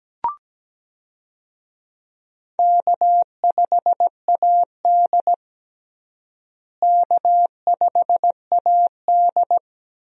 Morse code, heard as a keyed tone, about 17 wpm, 700 Hz. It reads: K5AD K5AD
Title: USA Call Signs - 4 Characters - Repeated 2x - 17wpm